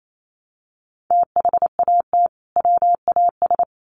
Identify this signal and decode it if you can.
Morse code, heard as a keyed tone, about 28 wpm, 700 Hz. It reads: THAT WAS
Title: Sets of 2 Words 28wpm